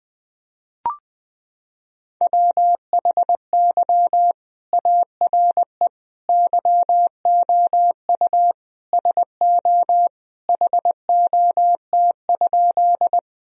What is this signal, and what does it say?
Morse code, heard as a keyed tone, about 19 wpm, 700 Hz. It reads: WHY ARE YOU SO HOT?
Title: Sentences from Top 1000 Words - Review Part 3 - 20wpm